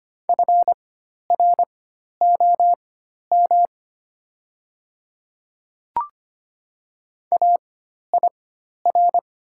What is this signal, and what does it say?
Morse code, heard as a keyed tone, about 25 wpm, spaced as for 12 wpm, 700 Hz. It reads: FROM AIR
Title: 4x Character Spacing - Top 300 Words - 25wpm